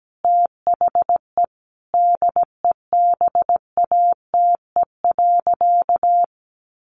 Morse code, heard as a keyed tone, about 17 wpm, 700 Hz. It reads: THE DEBATE.
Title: Sentences from Top 1500 Words 17wpm